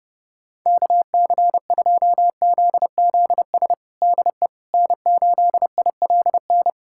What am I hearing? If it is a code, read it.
Morse code, heard as a keyed tone, about 30 wpm, 700 Hz. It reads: KC2ZZS DE N8ILN
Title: USA Callsign DE Callsign 30wpm